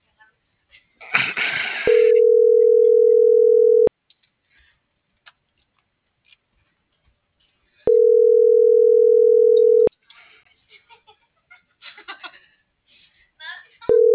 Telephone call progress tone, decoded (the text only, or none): ringback tone